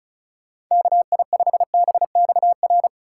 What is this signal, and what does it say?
Morse code, heard as a keyed tone, about 43 wpm, 700 Hz. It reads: KI5BXR
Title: Speed Racing - USA 2x3 Call Signs 35wpm